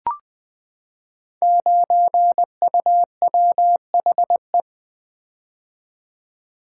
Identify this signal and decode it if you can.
Morse code, heard as a keyed tone, about 20 wpm, 700 Hz. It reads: 9UWHE
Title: Code Groups - 5 Letter:Numbers - 20wpm